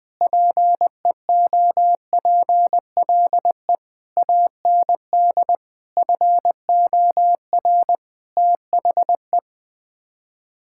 Morse code, heard as a keyed tone, about 21 wpm, 700 Hz. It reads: PEOPLE AND FOR THE
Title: Sets of 9 Words - 20wpm